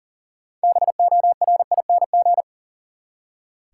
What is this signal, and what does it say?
Morse code, heard as a keyed tone, about 40 wpm, 700 Hz. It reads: BORING
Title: Six Letter Words - 40wpm